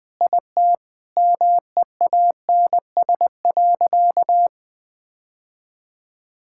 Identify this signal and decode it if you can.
Morse code, heard as a keyed tone, about 21 wpm, 700 Hz. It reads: IT MEANS.
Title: Sentences with 5-Word Phrases - Spoken Beginning - 20wpm